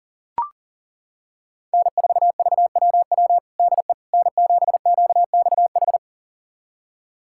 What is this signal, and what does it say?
Morse code, heard as a keyed tone, about 40 wpm, 700 Hz. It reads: N4VWW DE N7QXH
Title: USA Callsign DE Callsign - 5 Characters - 40wpm